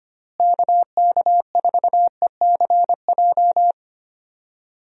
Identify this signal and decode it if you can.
Morse code, heard as a keyed tone, about 25 wpm, 700 Hz. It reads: KK4ECJ
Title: USA Call Signs - 6 Characters - Repeated 3x - 25wpm